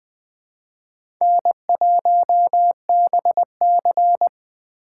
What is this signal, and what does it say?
Morse code, heard as a keyed tone, about 20 wpm, 700 Hz. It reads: N1BC
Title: USA Call Signs - 4 Characters - 3x Character Spacing - 20wpm